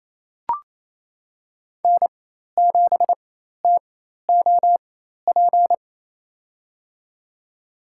Morse code, heard as a keyed tone, about 27 wpm, spaced as for 13 wpm, 700 Hz. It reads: N7TOP
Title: USA Call Signs - 4x Character Spacing - 28wpm